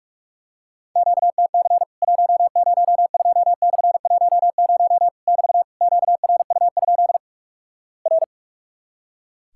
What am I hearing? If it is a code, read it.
Morse code, heard as a keyed tone, about 45 wpm, 700 Hz. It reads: QTC 102/10 <BT> QRU? R
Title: Worked All Europe DX Contest - QTC Exchange - 45wpm